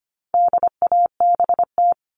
Morse code, heard as a keyed tone, about 25 wpm, 700 Hz. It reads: DABT
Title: All Cut Numbers - 4-Digit 25wpm